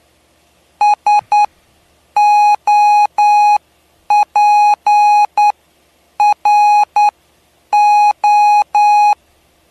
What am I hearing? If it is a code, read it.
Morse code, heard as a keyed tone, about 9 wpm, spaced as for 7 wpm, 830 Hz. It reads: SOPRO
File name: cw / py2lcd / 07-PY2LCD-Li��o 7